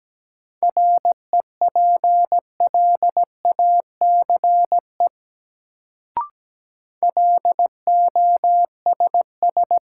Morse code, heard as a keyed tone, about 17 wpm, 700 Hz. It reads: REPLACE LOSS
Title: Sentences from Top 1400 Words - New Word Drill Alt - 17wpm